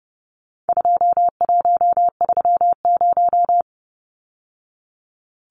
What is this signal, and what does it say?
Morse code, heard as a keyed tone, about 30 wpm, 700 Hz. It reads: 2130
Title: Numbers - 4-Digit 30wpm